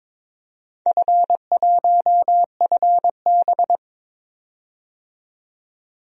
Morse code, heard as a keyed tone, about 22 wpm, 700 Hz. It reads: F1FB
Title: DXCC-100 Call Signs - 4 Characters - 22wpm